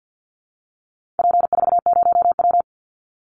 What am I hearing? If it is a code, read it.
Morse code, heard as a keyed tone, about 50 wpm, 700 Hz. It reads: P40W
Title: Call Signs Repeated 3x - 50wpm